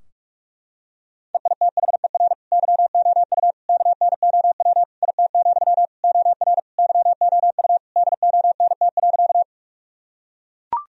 Morse code, heard as a keyed tone, about 41 wpm, 700 Hz. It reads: EITHER YOU KNOW IT, OR YOU DONT.
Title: Sentences from Top 1000 Words - Review Part 4 - 45wpm